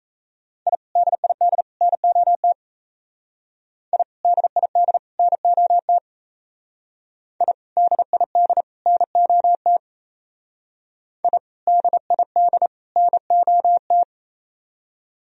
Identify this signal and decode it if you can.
Morse code, heard as a keyed tone, about 35 wpm, 700 Hz. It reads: I DID NOT I DID NOT I DID NOT I DID NOT
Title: Speed-Racing - Sets of 3 Words 28wpm